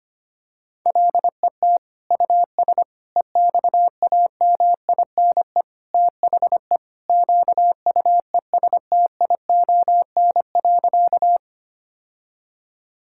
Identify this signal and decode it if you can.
Morse code, heard as a keyed tone, about 25 wpm, 700 Hz. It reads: LET US EXAMINE THE QUESTION.
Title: Sentences from Top 1600 Words 25wpm